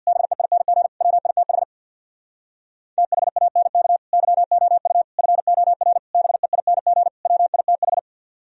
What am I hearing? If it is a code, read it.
Morse code, heard as a keyed tone, about 64 wpm, 700 Hz. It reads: BEING WITH THANK YOU FOR BEING WITH